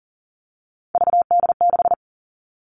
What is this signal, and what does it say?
Morse code, heard as a keyed tone, about 40 wpm, 700 Hz. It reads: VD6
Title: All Cut Numbers - 3-Digit 40wpm